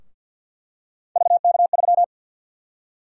Morse code, heard as a keyed tone, about 47 wpm, 700 Hz. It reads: VK3
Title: USA Call Sign Prefixes - Repeated 2x - 50wpm